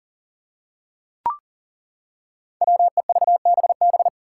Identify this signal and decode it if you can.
Morse code, heard as a keyed tone, about 40 wpm, 700 Hz. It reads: WEVBB
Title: Five Challenging Letter-Number Combinations 40wpm